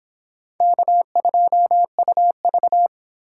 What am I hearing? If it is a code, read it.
Morse code, heard as a keyed tone, about 26 wpm, 700 Hz. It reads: K2UV